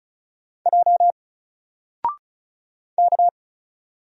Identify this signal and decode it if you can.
Morse code, heard as a keyed tone, about 35 wpm, 700 Hz. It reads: J K
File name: Single-Letter-Number-35wpm